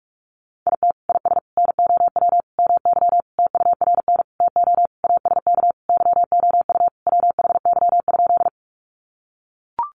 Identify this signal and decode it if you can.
Morse code, heard as a keyed tone, about 44 wpm, 700 Hz. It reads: IT IS NOW MY TURN TO ASK YOU WHY?
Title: Sentences from Top 200 Words 45wpm